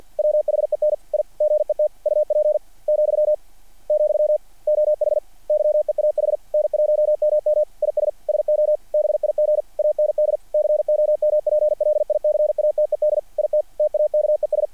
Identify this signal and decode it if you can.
Morse code, heard as a keyed tone, about 48 wpm, 660 Hz. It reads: WHEN I GET UP,, OH YEAH N1MM IS SO BIG AND COMPLICATED IT TAKES
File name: archived / CRSnet